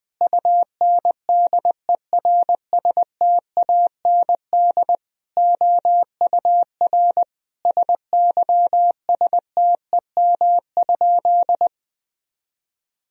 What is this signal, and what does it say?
Morse code, heard as a keyed tone, about 21 wpm, 700 Hz. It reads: UNDERSTAND OUR SYSTEM?
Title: Sentences from Top 1000 Words - Review Part 3 - 20wpm